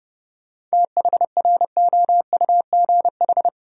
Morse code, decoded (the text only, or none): THROUGH